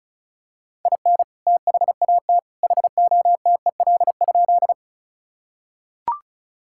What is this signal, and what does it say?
Morse code, heard as a keyed tone, about 34 wpm, 700 Hz. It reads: IN THAT HOTEL?
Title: Sentences from Top 1600 Words 35wpm